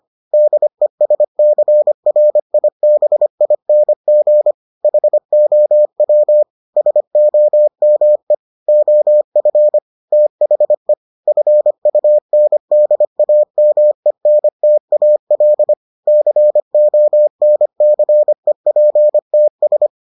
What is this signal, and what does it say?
Morse code, heard as a keyed tone, about 25 wpm, 600 Hz. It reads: DESCRIBING HOW SOME OF THE FUNDAMENTAL CONCEPTS